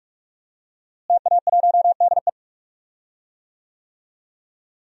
Morse code, heard as a keyed tone, about 44 wpm, 700 Hz. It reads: TA1DE
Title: DXCC-100 Call Signs - 5 Characters - Repeated 2x - 45wpm